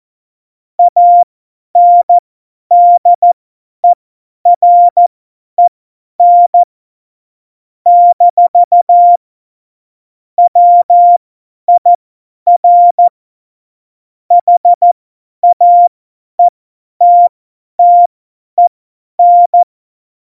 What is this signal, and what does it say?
Morse code, heard as a keyed tone, about 14 wpm, spaced as for 10 wpm, 700 Hz. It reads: ANDEREN - WIR HAETTEN
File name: Morsekurs / Witze / Witz 07 April 2020, 50 F70